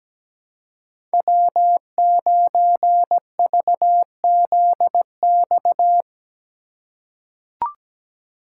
Morse code, heard as a keyed tone, about 17 wpm, 700 Hz. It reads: W9VZX
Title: USA Call Signs - 5 Characters - 3x Character Spacing - 17wpm